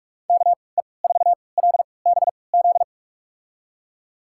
Morse code, heard as a keyed tone, about 46 wpm, spaced as for 26 wpm, 700 Hz. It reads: KE4LBZ
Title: USA Call Signs - 6 Characters - 3x Character Spacing - 45wpm